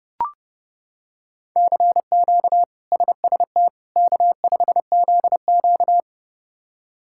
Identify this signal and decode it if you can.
Morse code, heard as a keyed tone, about 30 wpm, 700 Hz. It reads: CQ SST K5ZQ
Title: K1USN SST QSO Practice - 30wpm